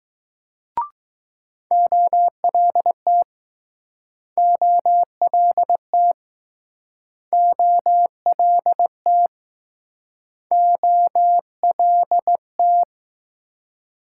Morse code, heard as a keyed tone, about 20 wpm, 700 Hz. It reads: OLT OLT OLT OLT